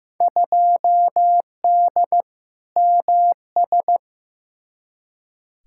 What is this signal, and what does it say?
Morse code, heard as a keyed tone, about 15 wpm, 700 Hz. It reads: 2D MS
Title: Field Day Exchanges Encore - 15wpm